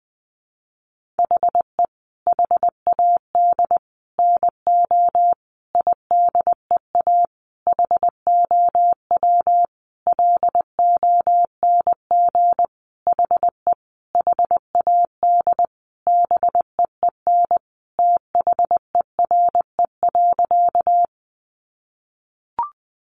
Morse code, heard as a keyed tone, about 20 wpm, 700 Hz. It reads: HE HAD NO IDEA HOW LONG HE HAD BEEN THERE.
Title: Sentences from Top 300 Words 20wpm